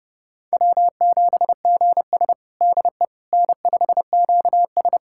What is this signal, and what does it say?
Morse code, heard as a keyed tone, about 30 wpm, 700 Hz. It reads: W7GS DE N5QS
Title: USA Callsign DE Callsign - 4 Characters - 30wpm